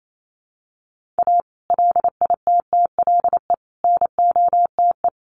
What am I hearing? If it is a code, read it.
Morse code, heard as a keyed tone, about 28 wpm, 700 Hz. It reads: A LITTLE NOTE
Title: Sentences from Top 1500 Words 28wpm